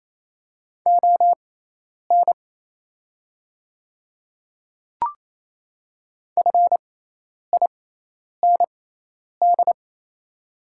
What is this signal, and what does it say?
Morse code, heard as a keyed tone, about 28 wpm, spaced as for 10 wpm, 690 Hz. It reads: ON FIND
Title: 6x Character Spacing - Top 100 Words - 28wpm